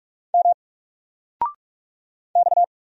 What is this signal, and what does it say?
Morse code, heard as a keyed tone, about 44 wpm, 700 Hz. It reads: M X
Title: Single Character (Letter:Number:Punctuation:Prosign) 45wpm